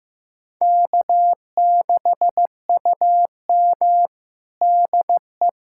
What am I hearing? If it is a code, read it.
Morse code, heard as a keyed tone, about 15 wpm, 700 Hz. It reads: K6UM DE
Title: USA Callsign DE Callsign - 4 Characters - 15wpm